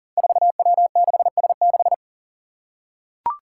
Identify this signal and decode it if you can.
Morse code, heard as a keyed tone, about 40 wpm, 700 Hz. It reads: 4W6S6
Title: Five Challenging Letter-Number Combinations 40wpm